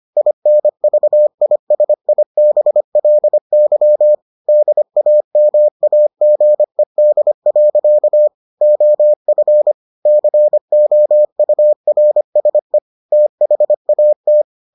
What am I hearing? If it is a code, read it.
Morse code, heard as a keyed tone, about 25 wpm, 600 Hz. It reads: INVISIBLY DAMAGED. OF COURSE THAT